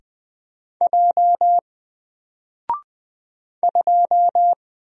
Morse code, heard as a keyed tone, about 20 wpm, 700 Hz. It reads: J 2
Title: Single Letter-Number Alt 20wpm